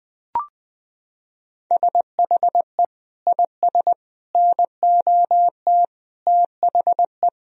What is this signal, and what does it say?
Morse code, heard as a keyed tone, about 20 wpm, 700 Hz. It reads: SHE IS NOT THE
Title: Sentences from Top 1000 Words - Review Part 2 - 20wpm